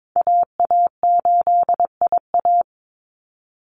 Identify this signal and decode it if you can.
Morse code, heard as a keyed tone, about 22 wpm, 700 Hz. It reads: AA8IA
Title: Call Signs Repeated 3x - 22wpm